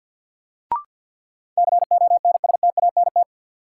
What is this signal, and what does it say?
Morse code, heard as a keyed tone, about 52 wpm, 700 Hz. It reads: CONSTANT
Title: Sentences from Top 1500 Words - New Word Drill - 50wpm